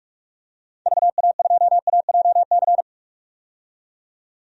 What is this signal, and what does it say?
Morse code, heard as a keyed tone, about 42 wpm, 700 Hz. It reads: VA2AJC
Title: DXCC-100 Call Signs - 6 Characters - 45wpm